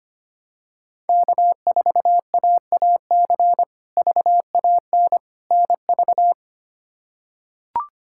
Morse code, heard as a keyed tone, about 25 wpm, 700 Hz. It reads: K4AAC VAN NV
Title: K1USN SST QSO Practice - 25wpm